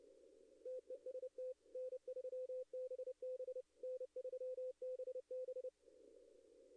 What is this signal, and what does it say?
Morse code, heard as a keyed tone, about 30 wpm, 510 Hz. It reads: TEST N3BB N3BB
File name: contests / arrl-160-k5tr-op-w5tn -2300